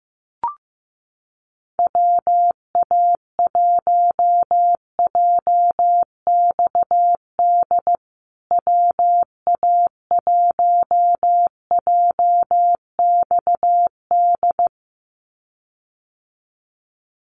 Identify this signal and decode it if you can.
Morse code, heard as a keyed tone, about 15 wpm, 700 Hz. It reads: WA1JXD WA1JXD
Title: USA Call Signs Repeated 2x 15wpm